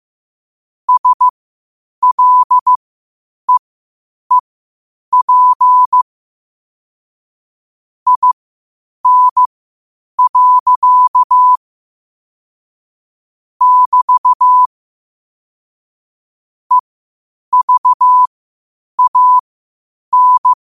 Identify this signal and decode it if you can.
Morse code, heard as a keyed tone, about 15 wpm, spaced as for 8 wpm, 1000 Hz. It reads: SLEEP IN. <BT> EVAN
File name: qotdfiles / 8wpm